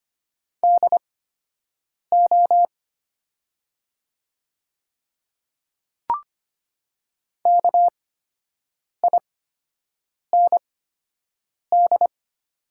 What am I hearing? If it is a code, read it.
Morse code, heard as a keyed tone, about 25 wpm, spaced as for 7 wpm, 700 Hz. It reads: DO KIND
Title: 8x Character Spacing - Top 200 Words - 25wpm